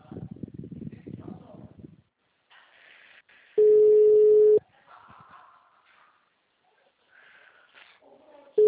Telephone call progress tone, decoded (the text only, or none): ringback tone